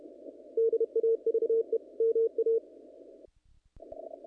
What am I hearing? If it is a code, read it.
Morse code, heard as a keyed tone, about 31 wpm, 450 Hz. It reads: DAVE MA
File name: contests / naqp-cw-k5tr-op-k5pi- -1900